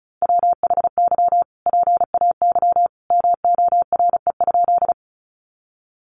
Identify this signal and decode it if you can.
Morse code, heard as a keyed tone, about 34 wpm, 700 Hz. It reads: WHY PAY MORE?
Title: Sentences from Top 700 Words 35wpm